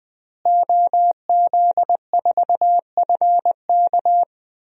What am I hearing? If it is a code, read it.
Morse code, heard as a keyed tone, about 20 wpm, 700 Hz. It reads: OZ4FK
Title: DXCC-100 Call Signs - 5 Characters - Repeated 2x - 20wpm